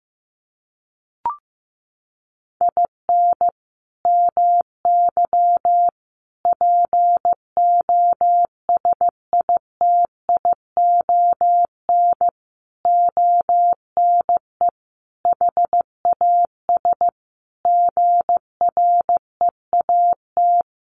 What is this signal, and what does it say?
Morse code, heard as a keyed tone, about 15 wpm, 700 Hz. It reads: IN MY POSITION ONE HAS GREAT
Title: Sentences from Top 1100 Words 15wpm